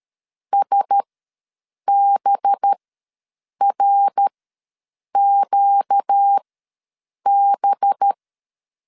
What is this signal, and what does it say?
Morse code, heard as a keyed tone, about 13 wpm, spaced as for 7 wpm, 800 Hz. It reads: SBRQB
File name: 5 LB ws 12wpm